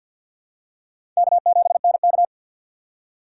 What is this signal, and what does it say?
Morse code, heard as a keyed tone, about 52 wpm, 700 Hz. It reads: K7NK
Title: USA Call Signs - 4 Characters - 3x Character Spacing - 50wpm